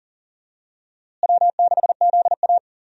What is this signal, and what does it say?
Morse code, heard as a keyed tone, about 40 wpm, 700 Hz. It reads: W6ZA
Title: USA Call Signs - 4 Characters - Repeated 2x - 40wpm